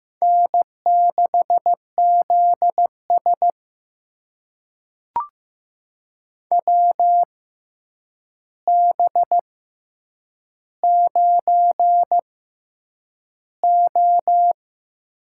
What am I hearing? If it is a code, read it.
Morse code, heard as a keyed tone, about 15 wpm, spaced as for 8 wpm, 700 Hz. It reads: N6ZS WB9O
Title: USA Call Signs - 4 Characters - 6x Character Spacing - 15wpm